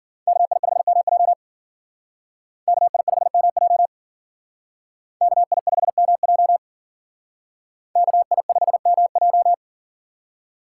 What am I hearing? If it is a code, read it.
Morse code, heard as a keyed tone, about 49 wpm, 700 Hz. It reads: KI5MJ KI5MJ KI5MJ KI5MJ
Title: Speed Racing - USA 2x2 Call Signs 40wpm